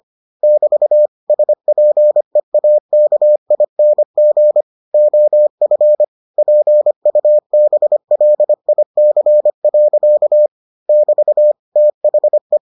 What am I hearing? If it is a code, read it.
Morse code, heard as a keyed tone, about 25 wpm, 600 Hz. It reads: <BT> SPEAKING OF PUBLIC. <BT> THE